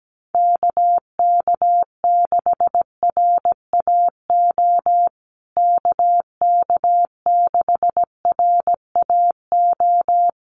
Morse code, heard as a keyed tone, about 17 wpm, 700 Hz. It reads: KK6RAO KK6RAO
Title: USA Call Signs Repeated 3x 17wpm